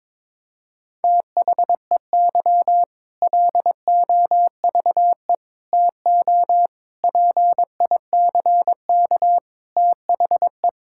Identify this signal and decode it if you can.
Morse code, heard as a keyed tone, about 21 wpm, 700 Hz. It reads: THEY LOVE TO PICK THE
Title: Sentences from Top 1600 Words 22wpm